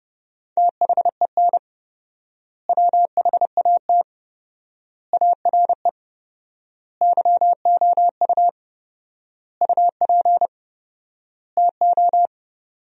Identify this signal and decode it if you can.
Morse code, heard as a keyed tone, about 32 wpm, 700 Hz. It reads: THEN WHAT ARE YOU UP TO
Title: 4x Word-Spacing - Sentences from Top 100 Words - 30wpm